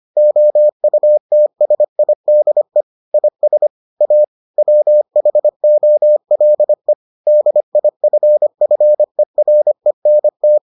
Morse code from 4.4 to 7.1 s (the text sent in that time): WHOLE